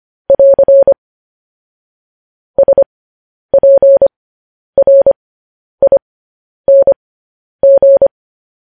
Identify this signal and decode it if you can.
Morse code, heard as a keyed tone, about 25 wpm, spaced as for 10 wpm, 550 Hz. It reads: <AR> SPRING
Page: news-headlines-in-morse-code-at-10-wpm